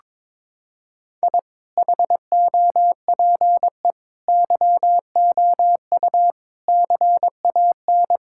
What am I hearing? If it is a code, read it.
Morse code, heard as a keyed tone, about 22 wpm, 700 Hz. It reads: I HOPE YOU CAN
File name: Sentences with 9-Word Phrases - 22wpm